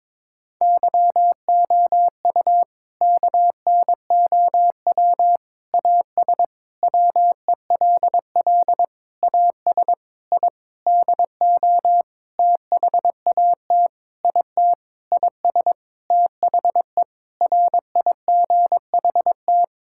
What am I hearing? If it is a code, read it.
Morse code, heard as a keyed tone, about 22 wpm, 700 Hz. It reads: YOU KNOW AS WELL AS I DO THAT IT IS THE RIGHT
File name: practice / 22wpm / Sentences with 8-Word Phrases - 22wpm